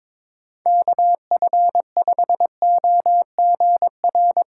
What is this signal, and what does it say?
Morse code, heard as a keyed tone, about 21 wpm, 700 Hz. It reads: KF5OGR
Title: USA Callsign DE Callsign - 6 Characters - 22wpm